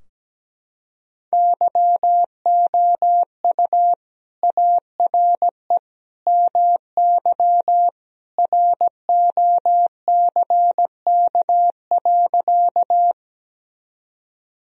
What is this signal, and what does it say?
Morse code, heard as a keyed tone, about 17 wpm, 700 Hz. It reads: YOU ARE MY ROCK.
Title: Sentences from Top 1000 Words - Review Part 2 - 17wpm